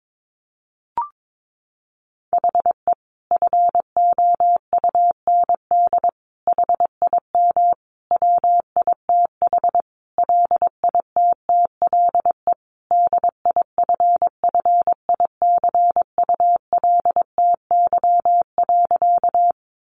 Morse code, heard as a keyed tone, about 22 wpm, 700 Hz. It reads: HE FOUND HIM WITH LITTLE DIFFICULTY.
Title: Sentences from Top 1500 Words 22wpm